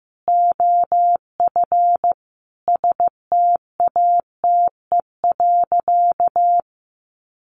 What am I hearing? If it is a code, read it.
Morse code, heard as a keyed tone, about 15 wpm, 700 Hz. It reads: OF STATE.
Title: Sentences from Top 1300 Words 15wpm